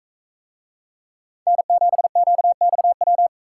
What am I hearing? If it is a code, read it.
Morse code, heard as a keyed tone, about 41 wpm, 700 Hz. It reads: N7QXW